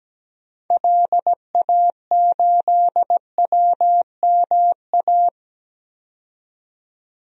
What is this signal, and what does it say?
Morse code, heard as a keyed tone, about 17 wpm, 700 Hz. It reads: LA8WMA
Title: DXCC-100 Call Signs - 6 Characters - 17wpm